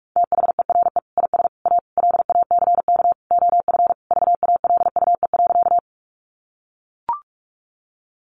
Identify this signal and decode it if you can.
Morse code, heard as a keyed tone, about 44 wpm, 700 Hz. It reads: THERE IS A LACK OF VALUE.
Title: Sentences from Top 1100 Words 45wpm